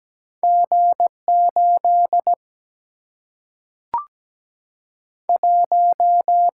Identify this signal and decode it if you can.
Morse code, heard as a keyed tone, about 17 wpm, 700 Hz. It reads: G 8 1
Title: Single-Letter Single-Number 17wpm